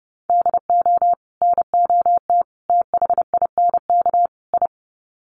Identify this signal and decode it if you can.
Morse code, heard as a keyed tone, about 28 wpm, 700 Hz. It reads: DO NOT THINK I
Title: Sets of 4 Words 30wpm